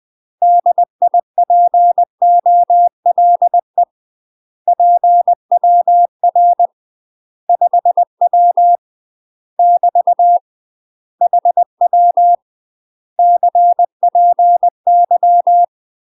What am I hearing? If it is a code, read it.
Morse code, heard as a keyed tone, about 20 wpm, 700 Hz. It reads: DIPOLE PWR 5W <BT> HW CPY
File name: UK MASTER QSOs TRAINING / 20WPM / UK MASTER QSOs 0045